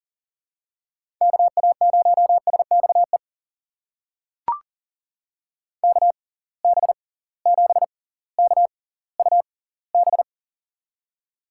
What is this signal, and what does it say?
Morse code, heard as a keyed tone, about 40 wpm, spaced as for 20 wpm, 700 Hz. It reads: KA0SXE KB7KUB
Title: USA Call Signs - 6x Character Spacing - 40wpm